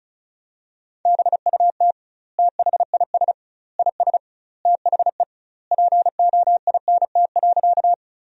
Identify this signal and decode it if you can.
Morse code, heard as a keyed tone, about 36 wpm, 700 Hz. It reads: BUT THIS IS THE POINT.
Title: 2x Word-Spacing - Sentences from Top 200 Words - 35wpm